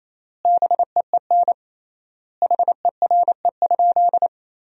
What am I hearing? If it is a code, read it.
Morse code, heard as a keyed tone, about 28 wpm, 700 Hz. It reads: BEEN HERE?
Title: 3x Word-Spacing - Sentences from Top 200 Words - 28wpm